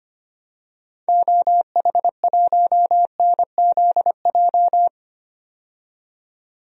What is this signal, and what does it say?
Morse code, heard as a keyed tone, about 25 wpm, 700 Hz. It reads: OH1NZJ
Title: DXCC-100 Call Signs - 6 Characters - 25wpm